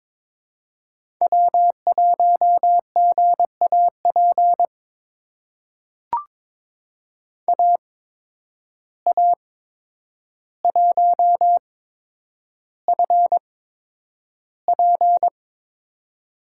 Morse code, heard as a keyed tone, about 22 wpm, spaced as for 9 wpm, 700 Hz. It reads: W1GAP AA1FP
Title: USA Call Signs - 5 Characters - 8x Character Spacing - 22wpm